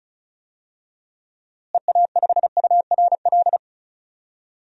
Morse code, heard as a keyed tone, about 36 wpm, 700 Hz. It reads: EA5URL
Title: DXCC-100 Call Signs - 6 Characters - Repeated 2x - 35wpm